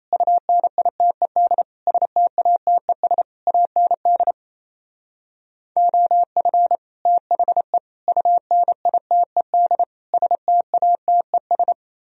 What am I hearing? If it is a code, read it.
Morse code, heard as a keyed tone, about 37 wpm, 700 Hz. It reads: UNITED STATES AND OF THE UNITED STATES